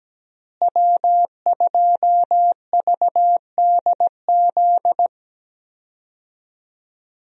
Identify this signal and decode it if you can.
Morse code, heard as a keyed tone, about 17 wpm, 700 Hz. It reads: W2VDZ
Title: USA Call Signs - 5 Characters - Repeated 2x - 17wpm